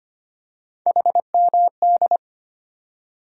Morse code, heard as a keyed tone, about 25 wpm, 700 Hz. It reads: HMD
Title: USA Call Sign Postfixes - 3 Letters 25wpm